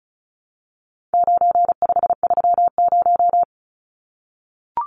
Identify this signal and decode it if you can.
Morse code, heard as a keyed tone, about 34 wpm, 700 Hz. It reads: 9530
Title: Numbers - 4-Digit 35wpm